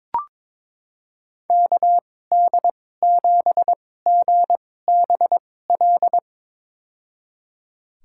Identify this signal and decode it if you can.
Morse code, heard as a keyed tone, about 21 wpm, spaced as for 16 wpm, 700 Hz. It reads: KD7GBL
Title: USA Call Signs - 6 Characters - 2x Character Spacing - 22wpm